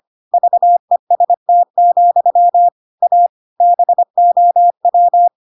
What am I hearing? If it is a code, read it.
Morse code, heard as a keyed tone, about 25 wpm, 700 Hz. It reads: VEST, A BOW